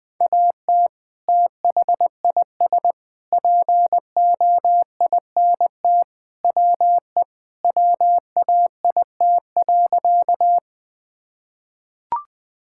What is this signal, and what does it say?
Morse code, heard as a keyed tone, about 20 wpm, 700 Hz. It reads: AT THIS POINT WE WAIT.
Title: Sentences with 3-Word Phrases - Spoken Beginning - 20wpm